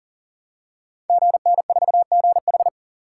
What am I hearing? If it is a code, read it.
Morse code, heard as a keyed tone, about 39 wpm, 700 Hz. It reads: GN4GH
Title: Code Groups - 5 Letter:Numbers - 40wpm